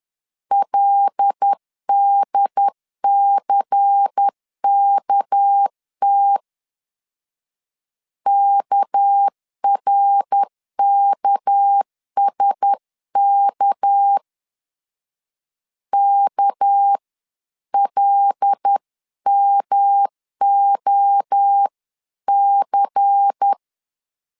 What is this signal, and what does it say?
Morse code, heard as a keyed tone, about 10 wpm, 800 Hz. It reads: LDCKT KRKSK KLMOC
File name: LEZIONI / 6 PKX0 / 6 PK 10wpm 2436